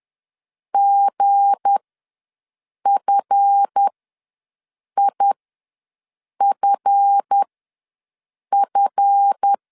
Morse code, heard as a keyed tone, about 10 wpm, spaced as for 6 wpm, 800 Hz. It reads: GFIFF
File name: LEZIONI / 7 GF9Z / 7 GF ws 10wpm 3918